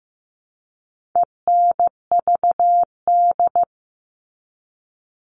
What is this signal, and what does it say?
Morse code, heard as a keyed tone, about 15 wpm, 700 Hz. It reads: ENVD
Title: All Cut Numbers - 4-Digit 15wpm